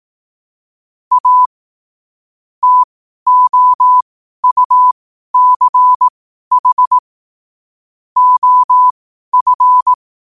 Morse code, heard as a keyed tone, about 18 wpm, spaced as for 12 wpm, 1000 Hz. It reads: A TOUCH OF
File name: qotdfiles / 12wpm